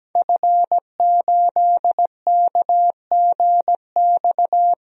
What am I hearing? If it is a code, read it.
Morse code, heard as a keyed tone, about 17 wpm, 700 Hz. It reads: F8KGX
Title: DXCC-100 Call Signs - 5 Characters - 17wpm